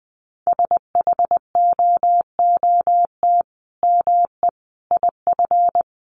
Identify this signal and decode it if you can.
Morse code, heard as a keyed tone, about 20 wpm, 700 Hz. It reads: SHOOT ME IF
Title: Sentences from Top 1200 Words 20wpm